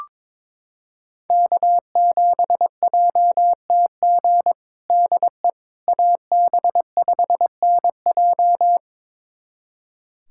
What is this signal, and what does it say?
Morse code, heard as a keyed tone, about 22 wpm, 700 Hz. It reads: K7JTG DE AB5NJ
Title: USA Callsign DE Callsign - 5 Characters - 22wpm